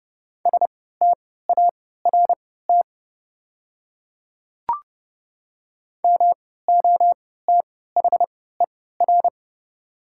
Morse code, heard as a keyed tone, about 30 wpm, spaced as for 17 wpm, 690 Hz. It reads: START MOTHER
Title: 3x Character Spacing - Top 300 Words - 30wpm